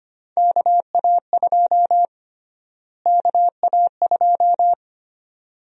Morse code, heard as a keyed tone, about 25 wpm, 700 Hz. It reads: KA2 KA2
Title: USA Call Sign Prefixes - Repeated 2x - 25wpm